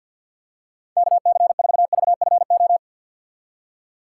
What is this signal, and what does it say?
Morse code, heard as a keyed tone, about 53 wpm, 700 Hz. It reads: KC4VFO
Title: USA Call Signs - 6 Characters - 4x Character Spacing - 50wpm